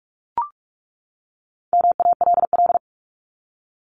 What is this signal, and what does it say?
Morse code, heard as a keyed tone, about 44 wpm, 700 Hz. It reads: MALL
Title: Top 1501-1750 Words 45wpm